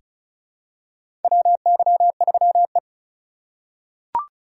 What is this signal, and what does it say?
Morse code, heard as a keyed tone, about 36 wpm, 700 Hz. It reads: WY3E
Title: USA Call Signs - 4 Characters - 2x Character Spacing - 35wpm